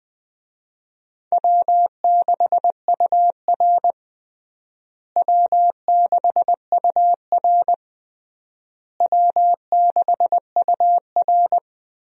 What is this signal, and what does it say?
Morse code, heard as a keyed tone, about 20 wpm, 700 Hz. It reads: W6UR W6UR W6UR
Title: USA Call Signs - 4 Characters - Repeated 3x - 20wpm